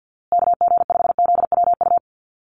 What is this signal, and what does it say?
Morse code, heard as a keyed tone, about 50 wpm, 700 Hz. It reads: KG5ZWU
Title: USA Call Signs Repeated 3x 50wpm